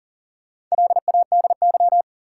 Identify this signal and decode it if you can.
Morse code, heard as a keyed tone, about 39 wpm, 700 Hz. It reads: LADY
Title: Four Letter Words - 40wpm